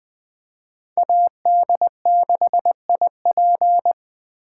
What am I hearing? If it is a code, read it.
Morse code, heard as a keyed tone, about 20 wpm, 700 Hz. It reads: AD6IP
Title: Speed Racing - USA 2x2 Call Signs 17wpm